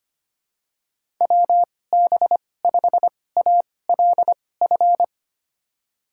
Morse code, heard as a keyed tone, about 25 wpm, spaced as for 18 wpm, 700 Hz. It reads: WB5ALF
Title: USA Call Signs - 6 Characters - 2x Character Spacing - 25wpm